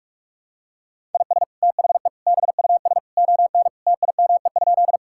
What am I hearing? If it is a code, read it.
Morse code, heard as a keyed tone, about 46 wpm, 700 Hz. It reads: IS THE BUS ON TIME?
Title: Sentences from Top 1000 Words - Review Part 6 - 45wpm